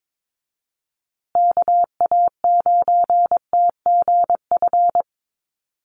Morse code, heard as a keyed tone, about 22 wpm, 700 Hz. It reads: KA9TGF
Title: USA Call Signs Repeated 2x 22wpm